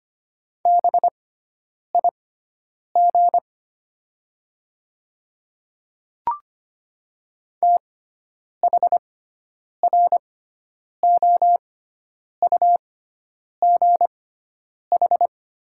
Morse code, heard as a keyed tone, about 25 wpm, spaced as for 9 wpm, 700 Hz. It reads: BIG THROUGH
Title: 6x Character Spacing - Top 200 Words - 25wpm